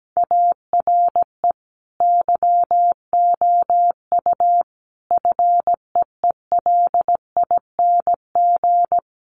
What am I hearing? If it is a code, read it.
Morse code, heard as a keyed tone, about 17 wpm, 700 Hz. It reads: ARE YOU FEELING
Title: Sentences from Top 1200 Words 17wpm